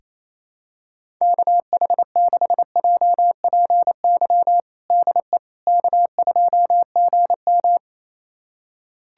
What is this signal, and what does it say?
Morse code, heard as a keyed tone, about 28 wpm, 700 Hz. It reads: KH6JPY DE K2GM
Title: USA Callsign DE Callsign 28wpm